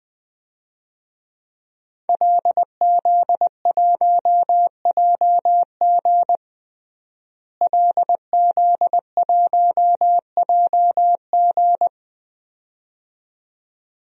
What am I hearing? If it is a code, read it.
Morse code, heard as a keyed tone, about 20 wpm, 700 Hz. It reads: LZ1JG LZ1JG
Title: DXCC-100 Call Signs - 5 Characters - Repeated 2x - 20wpm